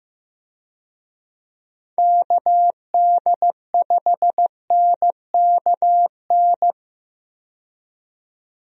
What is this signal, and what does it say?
Morse code, heard as a keyed tone, about 15 wpm, 700 Hz. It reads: KD5NKN
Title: USA Call Signs - 6 Characters - 15wpm